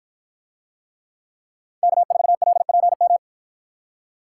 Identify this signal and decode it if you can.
Morse code, heard as a keyed tone, about 53 wpm, 700 Hz. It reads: K4LPM